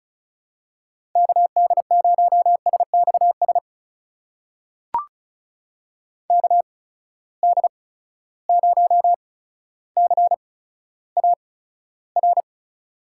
Morse code, heard as a keyed tone, about 36 wpm, spaced as for 15 wpm, 700 Hz. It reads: KD0SXS KD0CAR
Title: USA Call Signs - 6 Characters - 8x Character Spacing - 35wpm